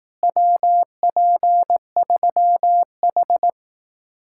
Morse code, heard as a keyed tone, about 18 wpm, 700 Hz. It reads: WP3H
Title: Speed-Racing Common QSO Phrases Encore - 15wpm